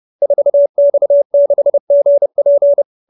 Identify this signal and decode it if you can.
Morse code, heard as a keyed tone, about 30 wpm, 590 Hz. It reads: 4X6GP
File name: QSO315 30